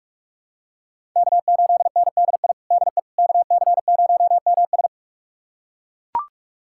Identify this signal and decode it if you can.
Morse code, heard as a keyed tone, about 42 wpm, 700 Hz. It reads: K8NDI DE KC0MS
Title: USA Callsign DE Callsign - 5 Characters - 45wpm